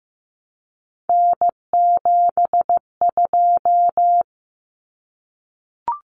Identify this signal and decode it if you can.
Morse code, heard as a keyed tone, about 15 wpm, 700 Hz. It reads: N72
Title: Common Cut Numbers - 3-Digit 15wpm